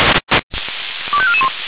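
Five-tone selective call (ZVEI), heard as quick runs of five tones, tones of about 70 ms. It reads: 25441
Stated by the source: ZVEI five-tone selective call received on SDR, from feuerwehr-orlamuende (Orlamünde fire brigade).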